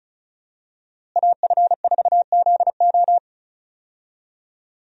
Morse code, heard as a keyed tone, about 36 wpm, 700 Hz. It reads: AF4ZO
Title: USA Call Signs - 5 Characters - Repeated 3x - 35wpm